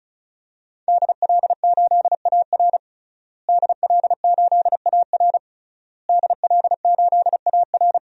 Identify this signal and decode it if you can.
Morse code, heard as a keyed tone, about 34 wpm, 700 Hz. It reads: DL8AR DL8AR DL8AR
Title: DXCC-100 Call Signs - 5 Characters - Repeated 3x - 35wpm